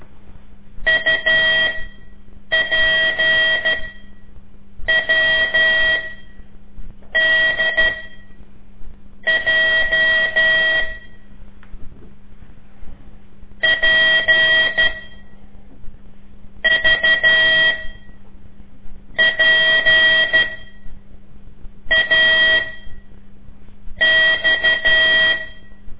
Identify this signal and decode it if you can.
Morse code, heard as a keyed tone, about 12 wpm, spaced as for 5 wpm, 1950 Hz. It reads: UPWDJ P VPAX
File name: ik1weg2 / CORSI CW / lez5-2